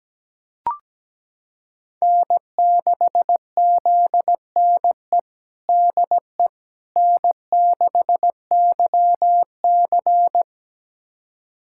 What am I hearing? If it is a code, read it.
Morse code, heard as a keyed tone, about 17 wpm, 700 Hz. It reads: N6ZNE DE N6YC
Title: USA Callsign DE Callsign 17wpm